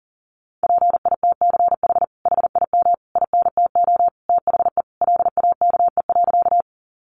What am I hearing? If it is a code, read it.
Morse code, heard as a keyed tone, about 40 wpm, 700 Hz. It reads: PITCH HIM INTO THE LAKE.
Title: Sentences from Top 1300 Words 40wpm